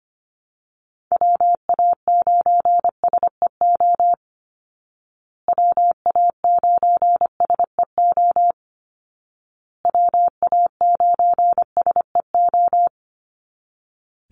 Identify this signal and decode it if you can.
Morse code, heard as a keyed tone, about 25 wpm, 700 Hz. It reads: WA9SEO WA9SEO WA9SEO
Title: Call Signs Repeated 2x - 25wpm